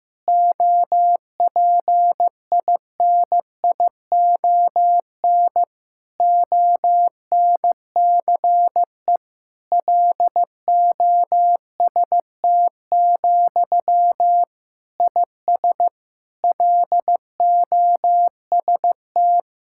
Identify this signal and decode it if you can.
Morse code, heard as a keyed tone, about 15 wpm, 700 Hz. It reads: OPINION ONCE LOST, IS LOST
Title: Sentences from Top 1600 Words 15wpm